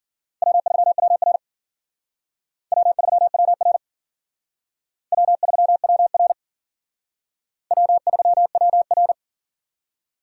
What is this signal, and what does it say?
Morse code, heard as a keyed tone, about 50 wpm, 700 Hz. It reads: W3WR W3WR W3WR W3WR
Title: Speed Racing - USA Call Signs 40wpm